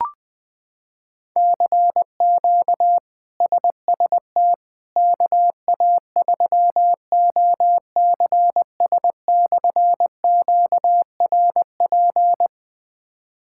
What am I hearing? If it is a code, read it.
Morse code, heard as a keyed tone, about 20 wpm, 700 Hz. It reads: CQ SST KA3OCS/QRP
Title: K1USN SST QSO Practice - 20wpm